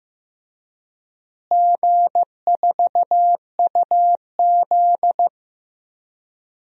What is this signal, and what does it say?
Morse code, heard as a keyed tone, about 15 wpm, 700 Hz. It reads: G4UZ
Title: DXCC-100 Call Signs - 4 Characters - Repeated 2x - 15wpm